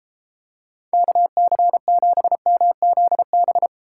Morse code, heard as a keyed tone, about 30 wpm, 700 Hz. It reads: KC7MZB